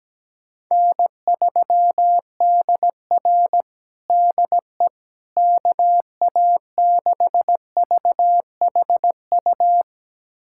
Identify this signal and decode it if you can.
Morse code, heard as a keyed tone, about 17 wpm, 700 Hz. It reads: N3DR DE KA6VHU
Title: USA Callsign DE Callsign 17wpm